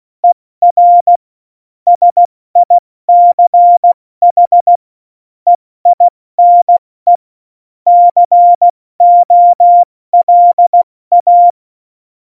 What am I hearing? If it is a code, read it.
Morse code, heard as a keyed tone, about 16 wpm, 700 Hz. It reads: ER SICH EINE COLA
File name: Morsekurs / Witze / Witz 08. Oktober 2019, 70 F80